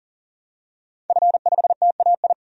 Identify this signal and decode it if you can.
Morse code, heard as a keyed tone, about 40 wpm, 700 Hz. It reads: F5TAI
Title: DXCC-100 Call Signs - 5 Characters - 40wpm